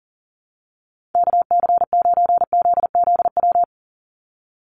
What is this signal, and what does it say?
Morse code, heard as a keyed tone, about 40 wpm, 700 Hz. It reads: KC9ZZW
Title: USA Call Signs 40wpm